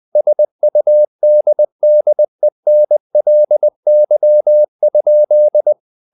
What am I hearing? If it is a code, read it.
Morse code, heard as a keyed tone, about 20 wpm, 600 Hz. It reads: SUDDENLY?
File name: the old man and the sea part 2 20wpm